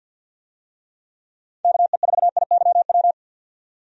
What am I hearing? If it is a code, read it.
Morse code, heard as a keyed tone, about 52 wpm, 700 Hz. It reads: KE4IYW
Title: USA Call Signs - 6 Characters - 50wpm